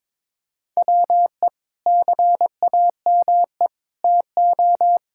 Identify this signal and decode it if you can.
Morse code, heard as a keyed tone, about 22 wpm, 700 Hz. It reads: WE CAME TO
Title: Sentences from Top 1000 Words - Review Part 6 - 22wpm